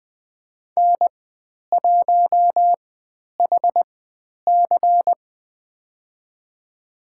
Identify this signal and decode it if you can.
Morse code, heard as a keyed tone, about 20 wpm, spaced as for 10 wpm, 700 Hz. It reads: N1HC
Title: 095 Lesson 16g - Letter c - Call Signs 20wpm